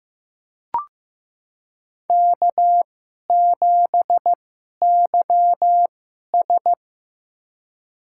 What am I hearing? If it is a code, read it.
Morse code, heard as a keyed tone, about 15 wpm, spaced as for 11 wpm, 700 Hz. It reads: K7YS
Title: USA Call Signs - 4 Characters - 2x Character Spacing - 15wpm